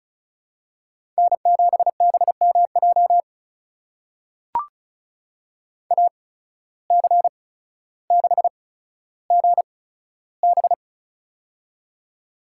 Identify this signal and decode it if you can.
Morse code, heard as a keyed tone, about 36 wpm, spaced as for 15 wpm, 700 Hz. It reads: N7BMJ AC6GB
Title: USA Call Signs - 5 Characters - 8x Character Spacing - 35wpm